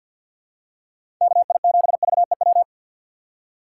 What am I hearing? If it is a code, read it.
Morse code, heard as a keyed tone, about 51 wpm, 700 Hz. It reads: KI7VEW
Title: USA Call Signs - 4x Character Spacing - 50wpm